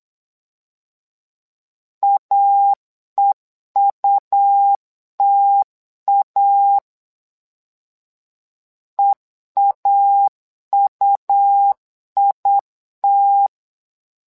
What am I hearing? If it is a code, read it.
Morse code, heard as a keyed tone, about 8 wpm, 800 Hz. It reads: AEUTA EAUIT
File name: LEZIONI / 2 UHDN / 2 UH 8wpm 2129